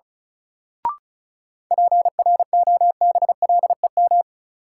Morse code, heard as a keyed tone, about 36 wpm, 700 Hz. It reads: PROBLEM
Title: Sentences from Top 1000 Words - New Word Drill - 35wpm